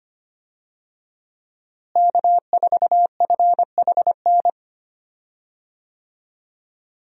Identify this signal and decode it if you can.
Morse code, heard as a keyed tone, about 25 wpm, 700 Hz. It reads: K4FHN
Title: USA Call Signs - 5 Characters - Repeated 2x - 25wpm